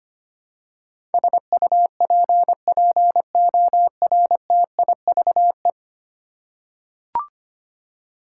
Morse code, heard as a keyed tone, about 25 wpm, 700 Hz. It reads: SUPPORTIVE
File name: Ten Letter Words - 25wpm